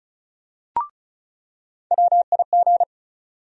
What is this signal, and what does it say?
Morse code, heard as a keyed tone, about 36 wpm, 700 Hz. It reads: WIG
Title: USA Call Sign Postfixes - 3 Letters 35wpm